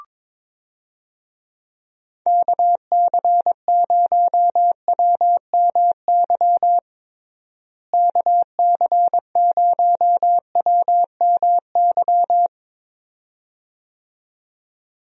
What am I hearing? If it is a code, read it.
Morse code, heard as a keyed tone, about 22 wpm, 700 Hz. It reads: KC0WMY KC0WMY
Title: USA Call Signs - 6 Characters - Repeated 2x - 22wpm